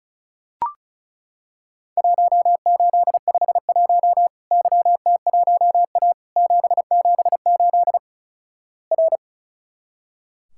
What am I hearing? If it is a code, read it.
Morse code, heard as a keyed tone, about 34 wpm, 700 Hz. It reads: 1851 YT1A 778 R
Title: Worked All Europe DX Contest - QTC Exchange - 35wpm